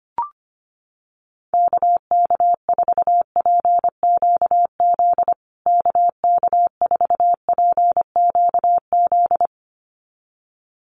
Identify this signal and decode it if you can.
Morse code, heard as a keyed tone, about 25 wpm, 700 Hz. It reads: KK4PQZ KK4PQZ
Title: USA Call Signs Repeated 2x 25wpm